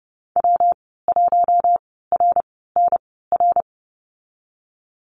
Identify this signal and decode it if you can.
Morse code, heard as a keyed tone, about 30 wpm, spaced as for 17 wpm, 700 Hz. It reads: W1RNR
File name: Lesson 19g - Letter w - Call Signs 30wpm